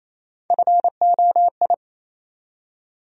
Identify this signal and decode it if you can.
Morse code, heard as a keyed tone, about 28 wpm, 700 Hz. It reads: FOI
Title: USA Call Sign Postfixes - 3 Letters 28wpm